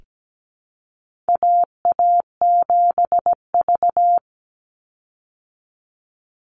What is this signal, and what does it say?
Morse code, heard as a keyed tone, about 17 wpm, 700 Hz. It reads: AA7V
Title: Call Signs Repeated 2x - 17wpm